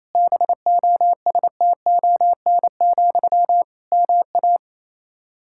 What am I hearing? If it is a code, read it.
Morse code, heard as a keyed tone, about 28 wpm, 700 Hz. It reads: BOSTON, MA
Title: Practice QSO - Exchange 1 - Beginning - 28wpm